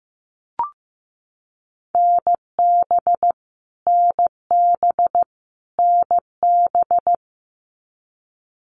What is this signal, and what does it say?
Morse code, heard as a keyed tone, about 15 wpm, 700 Hz. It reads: NB NB NB
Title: Canadian Provinces and Territories - Repeated 3x - 15wpm